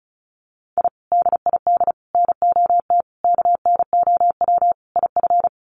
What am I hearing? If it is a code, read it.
Morse code, heard as a keyed tone, about 34 wpm, 700 Hz. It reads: I DID NOT KNOW IF
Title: Sets of 5 Words Encore 35wpm